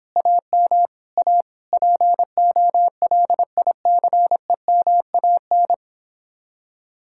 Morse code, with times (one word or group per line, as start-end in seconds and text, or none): 0.2-0.9 AM
1.2-1.4 A
1.7-5.7 POLICEMAN